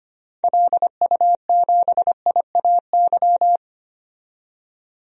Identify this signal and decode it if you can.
Morse code, heard as a keyed tone, about 25 wpm, 700 Hz. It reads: LU7IAY
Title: DXCC-100 Call Signs - 6 Characters - 25wpm